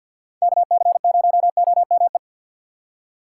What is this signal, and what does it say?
Morse code, heard as a keyed tone, about 45 wpm, 700 Hz. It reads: KC0OME
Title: USA Call Signs - 6 Characters - 3x Character Spacing - 50wpm